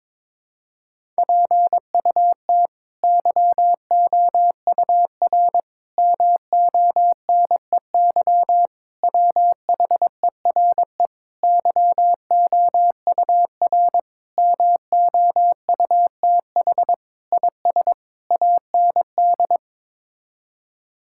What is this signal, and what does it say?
Morse code, heard as a keyed tone, about 22 wpm, 700 Hz. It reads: PUT YOUR MONEY WHERE YOUR MOUTH IS AND
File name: practice / 22wpm / Sets of 8 Words - 22wpm